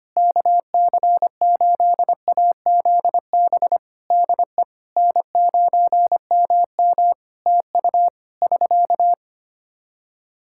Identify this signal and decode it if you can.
Morse code, heard as a keyed tone, about 25 wpm, 700 Hz. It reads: KC8AZB DE N9MM TU <SK>
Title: K1USN SST QSO Practice 25wpm